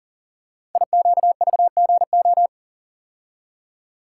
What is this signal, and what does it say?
Morse code, heard as a keyed tone, about 42 wpm, 690 Hz. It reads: IQVGO
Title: Code Groups - 5 Letters - 40wpm